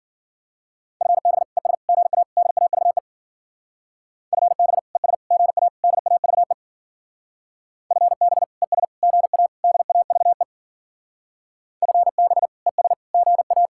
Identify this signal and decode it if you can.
Morse code, heard as a keyed tone, about 49 wpm, 700 Hz. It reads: FB ES GA DAVE FB ES GA DAVE FB ES GA DAVE FB ES GA
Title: Speed-Racing Common QSO Phrases Encore - 40wpm